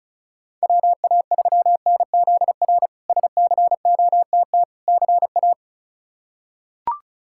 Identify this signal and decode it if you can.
Morse code, heard as a keyed tone, about 36 wpm, 700 Hz. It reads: WA3NZR SCOTT CA
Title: K1USN SST QSO Practice - 35wpm